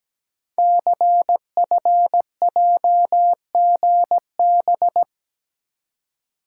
Morse code, heard as a keyed tone, about 17 wpm, 700 Hz. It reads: CFJGB
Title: Code Groups - 5 Letters - 17wpm